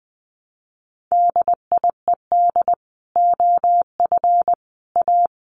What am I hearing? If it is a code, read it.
Morse code, heard as a keyed tone, about 20 wpm, 700 Hz. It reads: DIED OF A
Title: Sets of 5 Words Encore2 20wpm